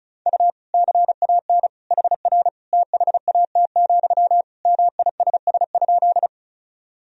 Morse code, heard as a keyed tone, about 34 wpm, 700 Hz. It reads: U CAN HR THAT, MISS?
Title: Sentences from Top 500 Words and Abbreviations 35wpm